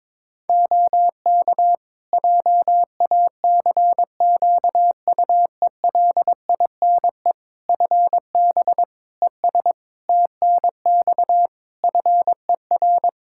Morse code, heard as a keyed tone, about 22 wpm, 700 Hz. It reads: OK JACQUELINE FB ES TNX FER
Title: K1USN SST QSO Practice 22wpm